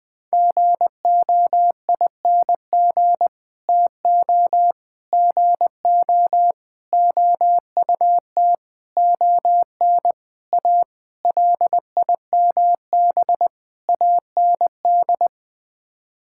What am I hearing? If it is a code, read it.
Morse code, heard as a keyed tone, about 21 wpm, 700 Hz. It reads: GOING TO GO OUT ON A LIMB AND
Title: Sets of 8 Words - 20wpm